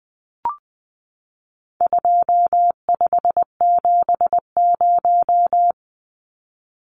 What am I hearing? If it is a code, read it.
Morse code, heard as a keyed tone, about 19 wpm, 700 Hz. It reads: 2570
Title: Numbers - 4-Digit 20wpm